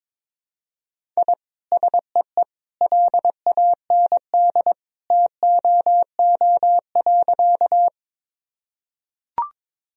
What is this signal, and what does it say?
Morse code, heard as a keyed tone, about 22 wpm, 700 Hz. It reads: I SEE LAND TOO.
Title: Sentences from Top 1000 Words - Review Part 1 - 22wpm